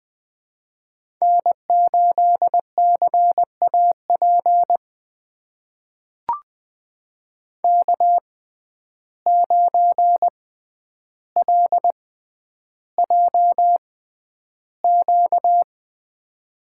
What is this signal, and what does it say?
Morse code, heard as a keyed tone, about 20 wpm, spaced as for 10 wpm, 700 Hz. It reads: N8CAP K9LJQ
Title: USA Call Signs - 5 Characters - 6x Character Spacing - 20wpm